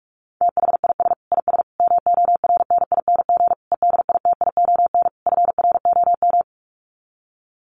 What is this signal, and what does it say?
Morse code, heard as a keyed tone, about 44 wpm, 700 Hz. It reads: THIS IS MORNING EDITION FROM
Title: Sets of 5 Words Encore2 45wpm